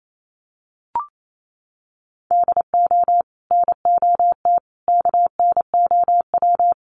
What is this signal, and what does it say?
Morse code, heard as a keyed tone, about 28 wpm, 700 Hz. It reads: DO NOT KNOW
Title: Sets of 5 Words 28wpm